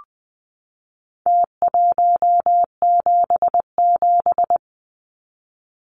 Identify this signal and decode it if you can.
Morse code, heard as a keyed tone, about 20 wpm, 700 Hz. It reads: T177
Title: Numbers - Contest Style - 4-Digit 20wpm